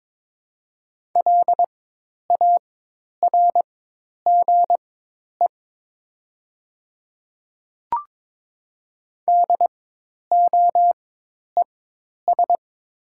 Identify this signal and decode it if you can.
Morse code, heard as a keyed tone, about 22 wpm, spaced as for 10 wpm, 700 Hz. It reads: LARGE DOES
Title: 4x Character Spacing - Top 200 Words - 22wpm